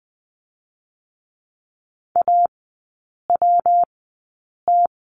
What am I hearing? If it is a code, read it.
Morse code, heard as a keyed tone, about 20 wpm, spaced as for 8 wpm, 700 Hz. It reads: AWT
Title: 114 Lesson 19e - Letter w - Three Characters 20wpm